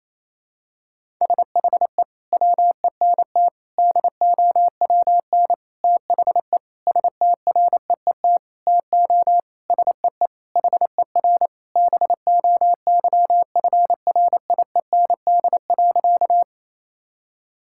Morse code, heard as a keyed tone, about 28 wpm, 700 Hz. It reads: SHE WENT DOWN THE STREET TO SEE HER BOYFRIEND.
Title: Sentences from Top 1000 Words - Review Part 5 - 28wpm